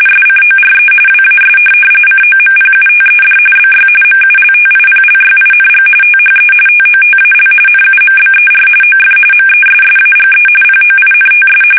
RTTY, about 45.45 baud. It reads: SCORE WAS COMPUTED BY MULTIPLYING THE TOTAL NUMBER OF MESSAGE POINTS
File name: fsk850 / teletype-RTTY